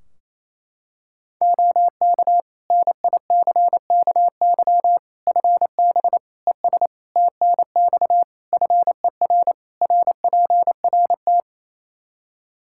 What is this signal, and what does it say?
Morse code, heard as a keyed tone, about 28 wpm, 700 Hz. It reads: OK NICKY FB ES TNX FER RPRT
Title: Practice QSO - Exchange 2 - Info - 28wpm